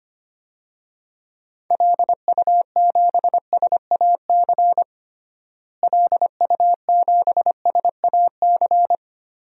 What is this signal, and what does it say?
Morse code, heard as a keyed tone, about 25 wpm, 700 Hz. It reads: LU7SAC LU7SAC
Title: DXCC-100 Call Signs - 6 Characters - Repeated 3x - 25wpm